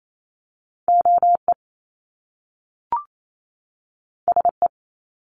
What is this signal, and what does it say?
Morse code, heard as a keyed tone, about 28 wpm, 700 Hz. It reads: OE SE
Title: Top Two Letter Combinations in Words 28wpm